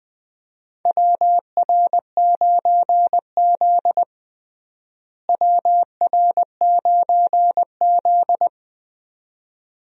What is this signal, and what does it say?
Morse code, heard as a keyed tone, about 20 wpm, 700 Hz. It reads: WR9Z WR9Z
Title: USA Call Signs - 4 Characters - Repeated 2x - 20wpm